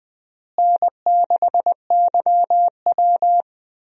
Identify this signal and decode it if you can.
Morse code, heard as a keyed tone, about 21 wpm, 700 Hz. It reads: N6YW
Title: USA Call Signs - 4 Characters - Repeated 2x - 20wpm